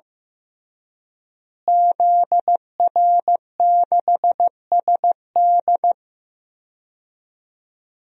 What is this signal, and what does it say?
Morse code, heard as a keyed tone, about 15 wpm, 700 Hz. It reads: ZR6SD
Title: DXCC-100 Call Signs - 5 Characters - Repeated 3x - 15wpm